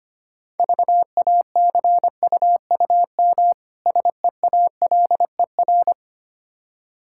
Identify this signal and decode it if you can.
Morse code, heard as a keyed tone, about 25 wpm, 700 Hz. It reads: VACUUM SEALER
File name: practice / 25wpm / Difficult Words - Speed-Racing with Hints - 25wpm